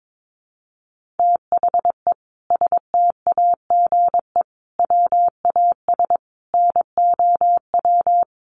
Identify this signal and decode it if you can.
Morse code, heard as a keyed tone, about 22 wpm, 700 Hz. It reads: THE STAGE WAS NOW
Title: Sentences from Top 1500 Words 22wpm